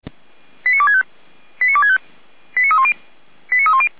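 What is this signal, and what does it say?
Five-tone selective call (ZVEI), heard as quick runs of five tones, tones of about 70 ms. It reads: 89265 89265 89310 89310
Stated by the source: ZVEI five-tone selective call received on an SDR.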